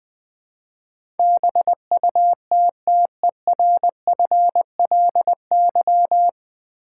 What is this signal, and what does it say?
Morse code, heard as a keyed tone, about 20 wpm, 700 Hz. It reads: BUTTERFLY
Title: Easy Words with Hints - 20wpm